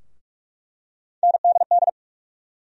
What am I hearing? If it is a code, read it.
Morse code, heard as a keyed tone, about 44 wpm, 700 Hz. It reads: NDD
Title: USA Call Sign Postfixes - 3 Letters 45wpm